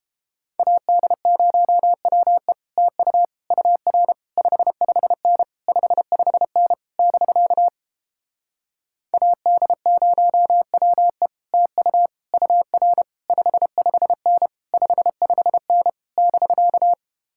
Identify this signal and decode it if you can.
Morse code, heard as a keyed tone, about 29 wpm, 700 Hz. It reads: AD0WE TU UR 55N 55N <BK> AD0WE TU UR 55N 55N <BK>